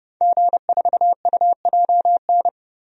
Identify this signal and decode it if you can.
Morse code, heard as a keyed tone, about 30 wpm, 700 Hz. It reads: G4UJN
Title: DXCC-100 Call Signs - 5 Characters - 30wpm